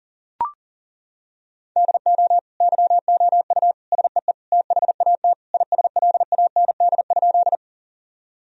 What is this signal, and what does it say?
Morse code, heard as a keyed tone, about 40 wpm, 700 Hz. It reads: DO YOU SEE THAT ISLAND?
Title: Sentences from Top 1000 Words - Review Part 3 - 40wpm